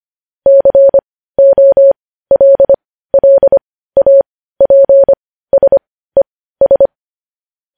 Morse code, heard as a keyed tone, about 25 wpm, spaced as for 15 wpm, 550 Hz. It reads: COLLAPSES